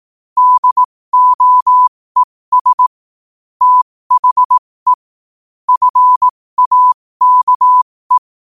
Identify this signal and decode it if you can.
Morse code, heard as a keyed tone, about 18 wpm, spaced as for 15 wpm, 1000 Hz. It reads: DOES THE FAKE
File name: qotdfiles / 15wpm